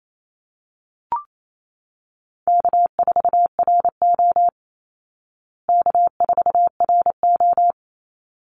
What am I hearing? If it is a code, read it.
Morse code, heard as a keyed tone, about 28 wpm, 700 Hz. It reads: K4RO K4RO
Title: Easy Call Signs Repeated 2x - 28wpm